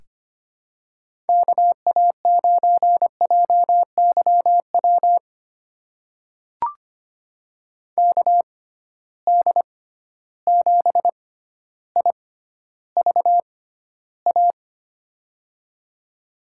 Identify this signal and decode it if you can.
Morse code, heard as a keyed tone, about 25 wpm, spaced as for 13 wpm, 700 Hz. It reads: KA9JYW KD7IVA
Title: USA Call Signs - 6x Character Spacing - 25wpm